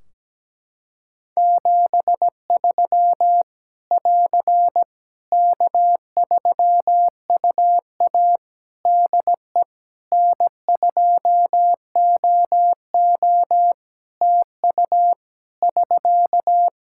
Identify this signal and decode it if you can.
Morse code, heard as a keyed tone, about 17 wpm, 700 Hz. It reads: 73 <AR> K3UA DE N2OO TU <SK>
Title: K1USN SST QSO Practice 17wpm